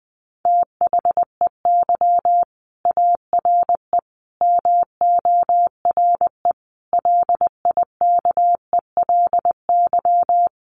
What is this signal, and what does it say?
Morse code, heard as a keyed tone, about 19 wpm, 700 Hz. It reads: THEY ARE MORE LIKELY
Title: Sets of 5 Words Encore2 20wpm